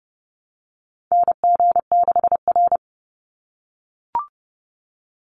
Morse code, heard as a keyed tone, about 30 wpm, 700 Hz. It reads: NG6R
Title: NA SOTA Chaser Callsigns 30wpm